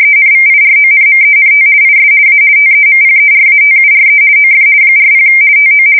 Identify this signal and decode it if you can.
RTTY, about 50 baud. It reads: ENVIRONMENT IS A TEMPERATURE OF 40 TO
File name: fsk170-50B / teletype-corp